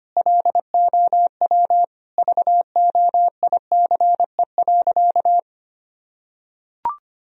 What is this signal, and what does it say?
Morse code, heard as a keyed tone, about 25 wpm, 700 Hz. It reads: LOW VOICE.
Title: Sentences from Top 1000 Words - Review Part 1 - 25wpm